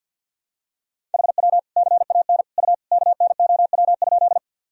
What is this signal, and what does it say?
Morse code, heard as a keyed tone, about 53 wpm, 700 Hz. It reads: HW CAN U KNOW?
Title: Sentences from Top 100 Words and Abbreviations 50wpm